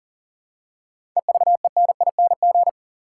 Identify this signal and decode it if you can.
Morse code, heard as a keyed tone, about 40 wpm, 700 Hz. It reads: EVENING
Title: Seven Letter Words - 40wpm